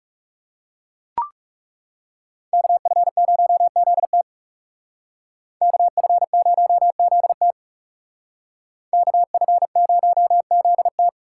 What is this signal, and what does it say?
Morse code, heard as a keyed tone, about 40 wpm, 700 Hz. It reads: KF0ZT KF0ZT KF0ZT